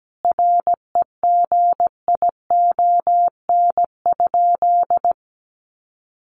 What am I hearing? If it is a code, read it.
Morse code, heard as a keyed tone, about 17 wpm, 700 Hz. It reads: REGION?
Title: Sentences from Top 1500 Words 17wpm